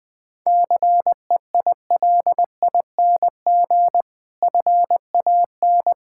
Morse code, heard as a keyed tone, about 20 wpm, 700 Hz. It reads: CEILING FAN
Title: Easy Words - Speed-Racing with Hints - 20wpm